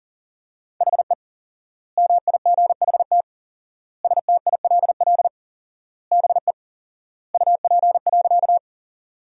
Morse code, heard as a keyed tone, about 40 wpm, 700 Hz. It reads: HE MIGHT STILL BE UP.
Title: 4x Word-Spacing - Sentences from Top 300 Words - 40wpm